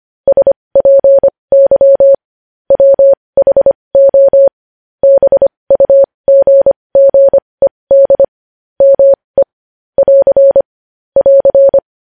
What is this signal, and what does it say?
Morse code, heard as a keyed tone, about 25 wpm, spaced as for 20 wpm, 550 Hz. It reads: SPY WHO BUGGED ME <AR> <AR>
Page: news-headlines-in-morse-code-at-20-wpm